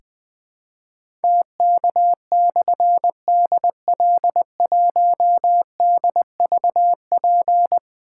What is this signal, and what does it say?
Morse code, heard as a keyed tone, about 20 wpm, 700 Hz. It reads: TK/DL1DVP
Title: Difficult Call Signs 20wpm